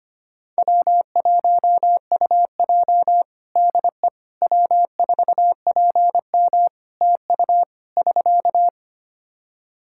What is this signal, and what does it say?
Morse code, heard as a keyed tone, about 25 wpm, 700 Hz. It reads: W1UJ DE W4PM TU <SK>
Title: K1USN SST QSO Practice 25wpm